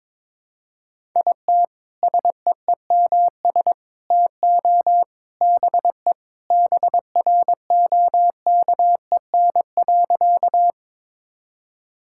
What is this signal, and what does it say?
Morse code, heard as a keyed tone, about 22 wpm, 700 Hz. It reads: IT SEEMS TO BE BROKEN.
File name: practice / 22wpm / Sentences with 3-Word Phrases - 22wpm